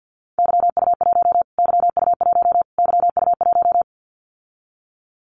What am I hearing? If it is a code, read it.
Morse code, heard as a keyed tone, about 49 wpm, 700 Hz. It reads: YU1 YU1 YU1
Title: Callsign Prefixes for Top 25 DXCC - Repeated 3x - 50wpm